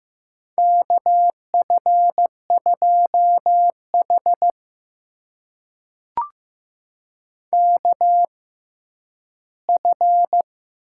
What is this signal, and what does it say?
Morse code, heard as a keyed tone, about 15 wpm, spaced as for 10 wpm, 700 Hz. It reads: KF2H KF
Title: USA Call Signs - 6x Character Spacing - 15wpm